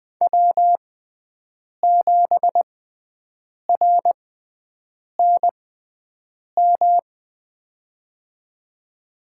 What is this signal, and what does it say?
Morse code, heard as a keyed tone, about 20 wpm, spaced as for 7 wpm, 700 Hz. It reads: W7RNM
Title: USA Call Signs - 5 Characters - 6x Character Spacing - 20wpm